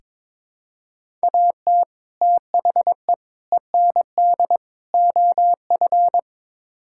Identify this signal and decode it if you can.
Morse code, heard as a keyed tone, about 21 wpm, 700 Hz. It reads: AT THE END OF
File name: Build A Phrase - 5-Word Phrases - 22wpm